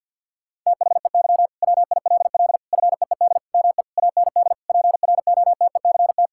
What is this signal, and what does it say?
Morse code, heard as a keyed tone, about 50 wpm, 700 Hz. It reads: THEY WILL FEED ME AND PROTECT
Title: Sentences from Top 1600 Words 50wpm